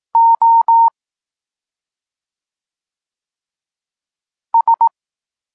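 Morse code, heard as a keyed tone, about 18 wpm, 920 Hz. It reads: O S
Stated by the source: Polish Morse lesson teaching the letter C, from SP9MOA Niepołomicki Klub Krótkofalowców.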